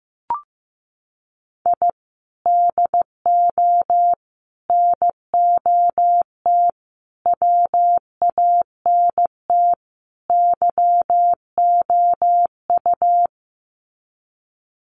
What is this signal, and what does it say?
Morse code, heard as a keyed tone, about 15 wpm, 700 Hz. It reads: I DO NOT WANT YOU
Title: Sets of 5 Words 15wpm